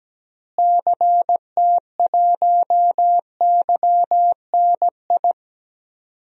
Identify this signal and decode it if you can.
Morse code, heard as a keyed tone, about 17 wpm, 700 Hz. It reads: CT1YNI
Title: DXCC-100 Call Signs - 6 Characters - Repeated 2x - 17wpm